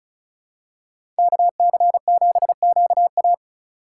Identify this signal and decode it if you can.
Morse code, heard as a keyed tone, about 36 wpm, 700 Hz. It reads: KC7QA
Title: USA Call Signs - 5 Characters - 3x Character Spacing - 35wpm